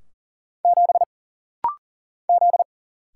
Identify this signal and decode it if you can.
Morse code, heard as a keyed tone, about 40 wpm, 700 Hz. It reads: 7 Z
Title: Single Letter-Number - Rapid-Fire 40wpm